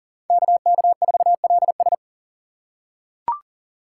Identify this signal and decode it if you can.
Morse code, heard as a keyed tone, about 40 wpm, 700 Hz. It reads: KK4LS
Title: USA Call Signs - 5 Characters - 2x Character Spacing - 40wpm